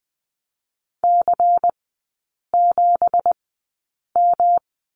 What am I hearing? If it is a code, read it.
Morse code, heard as a keyed tone, about 20 wpm, spaced as for 8 wpm, 700 Hz. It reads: C7M
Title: 166 Lesson 27e - Number 7 - Three Characters 20wpm